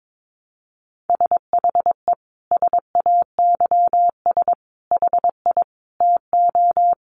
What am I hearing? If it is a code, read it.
Morse code, heard as a keyed tone, about 22 wpm, 700 Hz. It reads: SHE SAYS HI TO
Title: Easy Phrases 22wpm